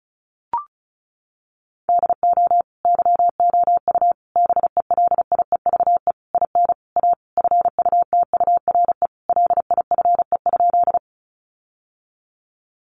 Morse code, heard as a keyed tone, about 34 wpm, 700 Hz. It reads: DO YOU BELIEVE IN A FUTURE LIFE?
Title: Sentences from Top 1000 Words 35wpm